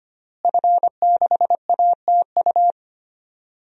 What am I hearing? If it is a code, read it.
Morse code, heard as a keyed tone, about 25 wpm, 700 Hz. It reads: F6ATU
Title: DXCC-100 Call Signs - 5 Characters - Repeated 3x - 25wpm